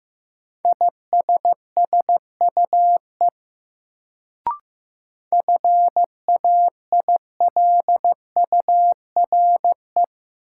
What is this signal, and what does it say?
Morse code, heard as a keyed tone, about 15 wpm, 700 Hz. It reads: ISSUE FAILURE
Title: Sentences from Top 900 Words - New Word Drill - 15wpm